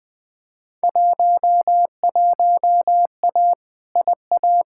Morse code, heard as a keyed tone, about 21 wpm, 700 Hz. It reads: 11A IA
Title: Field Day Exchanges Encore - 20wpm